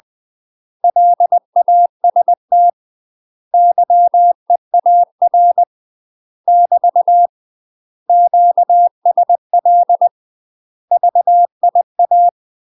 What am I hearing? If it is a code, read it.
Morse code, heard as a keyed tone, about 20 wpm, 700 Hz. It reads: LAST YEAR <BT> QSL VIA
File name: UK MASTER QSOs TRAINING / 20WPM / UK MASTER QSOs 0042